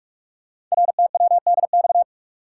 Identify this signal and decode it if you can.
Morse code, heard as a keyed tone, about 44 wpm, 700 Hz. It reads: RTWDX
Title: Code Groups - 5 Letters - 45wpm